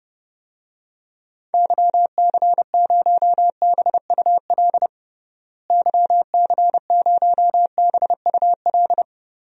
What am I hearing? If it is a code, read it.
Morse code, heard as a keyed tone, about 30 wpm, 700 Hz. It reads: YC0BUL YC0BUL
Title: DXCC-100 Call Signs - 6 Characters - Repeated 3x - 30wpm